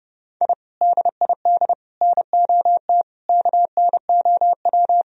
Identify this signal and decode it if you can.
Morse code, heard as a keyed tone, about 30 wpm, 700 Hz. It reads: I DID NOT KNOW
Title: Sentences from Top 1000 Words - Review Part 3 - 30wpm